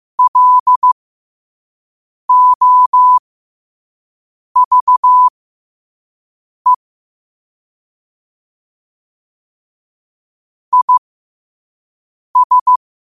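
Morse code, heard as a keyed tone, about 15 wpm, spaced as for 5 wpm, 1000 Hz. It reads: LOVE IS
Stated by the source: 5 WPM Morse quote of the day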